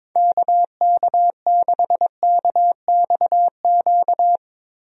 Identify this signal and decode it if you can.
Morse code, heard as a keyed tone, about 22 wpm, 700 Hz. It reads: KK6KXQ
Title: USA Call Signs - 6 Characters - Repeated 2x - 22wpm